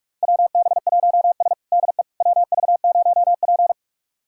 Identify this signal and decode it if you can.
Morse code, heard as a keyed tone, about 45 wpm, 700 Hz. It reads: WB1S DE WV0P
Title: USA Callsign DE Callsign - 4 Characters - 45wpm